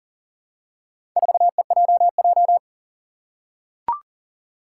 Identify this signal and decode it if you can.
Morse code, heard as a keyed tone, about 38 wpm, 700 Hz. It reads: 4EJJ
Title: Four Challenging Letter-Number Combinations 40wpm